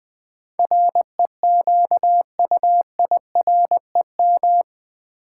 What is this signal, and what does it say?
Morse code, heard as a keyed tone, about 20 wpm, 700 Hz. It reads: REQUIREM
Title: Build A Word - 4 Letters At A Time - 11-Letter Words - 20wpm